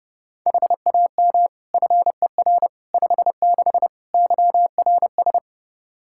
Morse code, heard as a keyed tone, about 30 wpm, 700 Hz. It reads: HAM FER 56 YRS
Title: Practice QSO - Exchange 3 - More Info - 30wpm